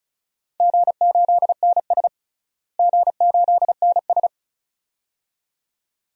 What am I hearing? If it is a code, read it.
Morse code, heard as a keyed tone, about 36 wpm, 690 Hz. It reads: G8NS G8NS
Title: DXCC-100 Call Signs - 4 Characters - Repeated 2x - 35wpm